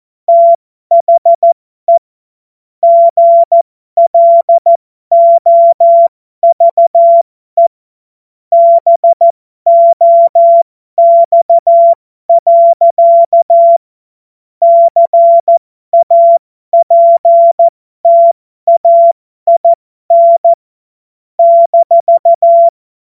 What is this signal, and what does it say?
Morse code, heard as a keyed tone, about 14 wpm, spaced as for 12 wpm, 680 Hz. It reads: THE GLOVE BOX. CAPTAIN -